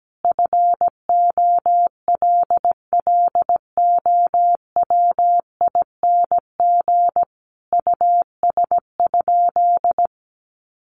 Morse code, heard as a keyed tone, about 17 wpm, 700 Hz. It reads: FOLLOWING US?
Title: Sentences from Top 700 Words 17wpm